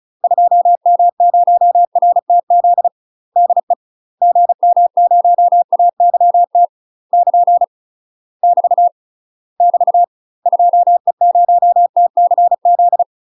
Morse code, heard as a keyed tone, about 36 wpm, 700 Hz. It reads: 2M0RTZ DE GM0AYT <KN> <BT> <BT> 2E0TCZ
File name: UK MASTER QSOs TRAINING / 35WPM / UK MASTER QSOs 0038